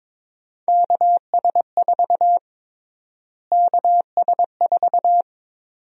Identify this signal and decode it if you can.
Morse code, heard as a keyed tone, about 22 wpm, 700 Hz. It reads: KS4 KS4
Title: USA Call Sign Prefixes - Repeated 2x - 22wpm